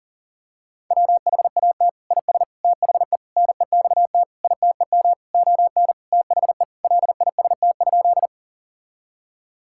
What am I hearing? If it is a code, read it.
Morse code, heard as a keyed tone, about 40 wpm, 700 Hz. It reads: WHAT IS THE NEXT ITEM ON THE LIST?
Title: Sentences from Top 1000 Words - Review Part 6 - 40wpm